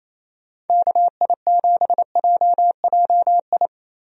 Morse code, heard as a keyed tone, about 28 wpm, 700 Hz. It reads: KI7JJI
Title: USA Call Signs - 6 Characters - 28wpm